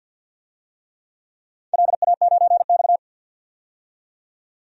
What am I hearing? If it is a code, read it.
Morse code, heard as a keyed tone, about 53 wpm, 700 Hz. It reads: LA9X
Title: DXCC-100 Call Signs - 4 Characters - 50wpm